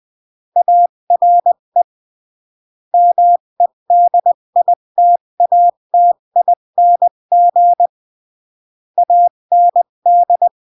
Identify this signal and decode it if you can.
Morse code, heard as a keyed tone, about 20 wpm, 700 Hz. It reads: ARE MEDITATING AND
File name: CW / jodorowsky / 82 commandments 20-15